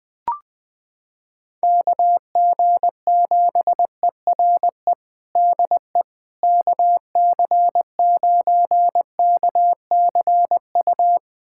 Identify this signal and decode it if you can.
Morse code, heard as a keyed tone, about 21 wpm, 700 Hz. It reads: KG7ERE DE KC9KCU
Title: USA Callsign DE Callsign 20wpm